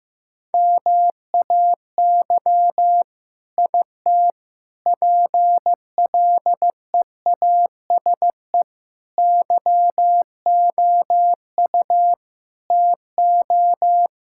Morse code, heard as a keyed tone, about 16 wpm, 700 Hz. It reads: MAY IT PLEASE YOU TO
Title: Sentences from Top 1600 Words 15wpm